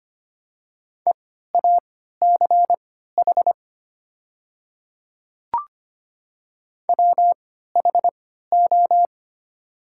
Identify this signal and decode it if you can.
Morse code, heard as a keyed tone, about 25 wpm, spaced as for 14 wpm, 700 Hz. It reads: EACH WHO
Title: 3x Character Spacing - Top 100 Words - 25wpm